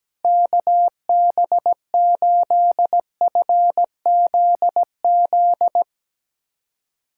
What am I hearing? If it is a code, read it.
Morse code, heard as a keyed tone, about 17 wpm, 700 Hz. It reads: KB8FZZ
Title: USA Callsign DE Callsign 17wpm